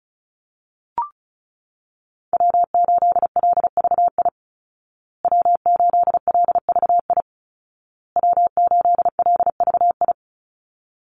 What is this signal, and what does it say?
Morse code, heard as a keyed tone, about 34 wpm, 700 Hz. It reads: W8LVI W8LVI W8LVI
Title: Call Signs Repeated 2x - 35wpm